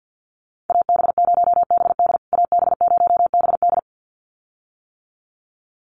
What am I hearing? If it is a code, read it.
Morse code, heard as a keyed tone, about 50 wpm, 700 Hz. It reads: AB0BD AB0BD
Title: USA Call Signs Repeated 2x 50wpm